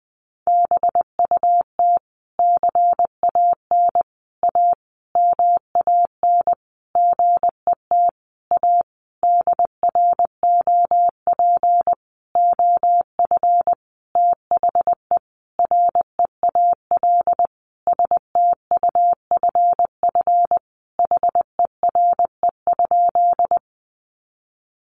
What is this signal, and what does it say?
Morse code, heard as a keyed tone, about 20 wpm, 700 Hz. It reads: BUT CAN A MAN GET A DROP OF THE REAL STUFF HERE?
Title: Sentences from Top 1400 Words 20wpm